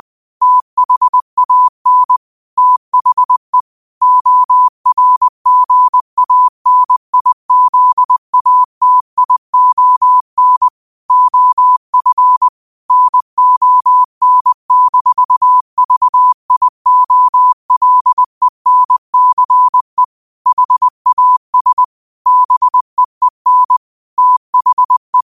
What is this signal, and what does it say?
Morse code, heard as a keyed tone, about 20 wpm, 1000 Hz. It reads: THAN THE ORGANIZATION OF NON-VIOLENCE HAS BEEN THE
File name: qotdfiles / 20wpm